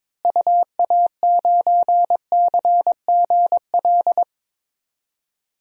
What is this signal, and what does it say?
Morse code, heard as a keyed tone, about 22 wpm, 700 Hz. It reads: UA9CGL
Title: DXCC-100 Call Signs - 6 Characters - Repeated 3x - 22wpm